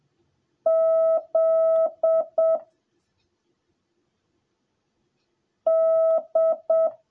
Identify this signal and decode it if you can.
Morse code, heard as a keyed tone, about 7 wpm, 650 Hz. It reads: Z D
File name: jh8cbh / 2021cw / cw10a